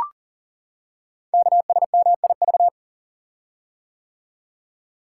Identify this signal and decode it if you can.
Morse code, heard as a keyed tone, about 41 wpm, 700 Hz. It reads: KSMIV
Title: Code Groups - 5 Letters - 40wpm